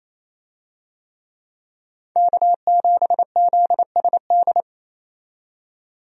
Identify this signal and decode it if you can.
Morse code, heard as a keyed tone, about 28 wpm, 700 Hz. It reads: K7ZSD
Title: USA Call Signs - 5 Characters - 28wpm